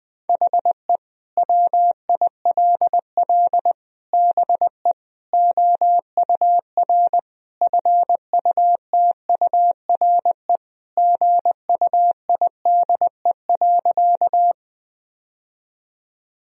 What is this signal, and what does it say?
Morse code, heard as a keyed tone, about 20 wpm, 700 Hz. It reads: HE WILL BE OUR FUTURE GUIDE.
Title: Sentences from Top 1000 Words - Review Part 3 - 20wpm